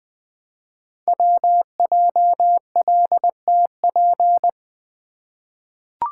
Morse code, heard as a keyed tone, about 20 wpm, 700 Hz. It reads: WJLTP
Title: Code Groups - 5 Letters - 20wpm